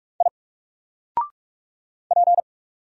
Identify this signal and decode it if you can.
Morse code, heard as a keyed tone, about 44 wpm, 700 Hz. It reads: I P
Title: Single Character (Letter:Number:Punctuation:Prosign) Alt 45wpm